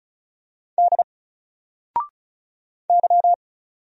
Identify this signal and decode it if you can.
Morse code, heard as a keyed tone, about 36 wpm, 700 Hz. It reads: D Y
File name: Single-Letter-Number-35wpm